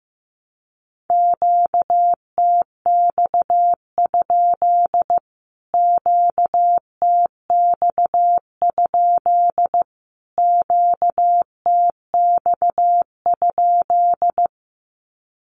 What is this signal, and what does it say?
Morse code, heard as a keyed tone, about 15 wpm, 700 Hz. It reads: QTX? QTX? QTX?
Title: Q-Codes Repeated 3x 15wpm